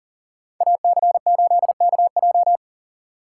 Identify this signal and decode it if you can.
Morse code, heard as a keyed tone, about 38 wpm, 700 Hz. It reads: AC8KJ
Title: USA Call Signs - 5 Characters - 2x Character Spacing - 40wpm